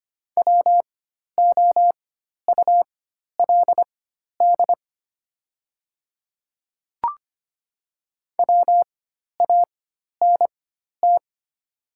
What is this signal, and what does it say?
Morse code, heard as a keyed tone, about 25 wpm, spaced as for 12 wpm, 700 Hz. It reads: WOULD WANT
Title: 4x Character Spacing - Top 200 Words - 25wpm